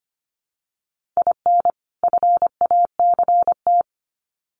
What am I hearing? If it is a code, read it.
Morse code, heard as a keyed tone, about 25 wpm, 700 Hz. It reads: IN FACT
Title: Sets of 2 Words 25wpm